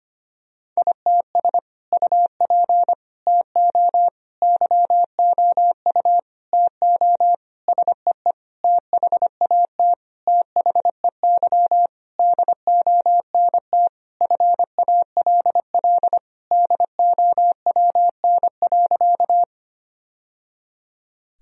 Morse code, heard as a keyed tone, about 25 wpm, 700 Hz. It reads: ITS UP TO YOU TO SEE THAT THEY DONT FALL DOWN.
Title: Sentences from Top 1000 Words - Review Part 2 - 25wpm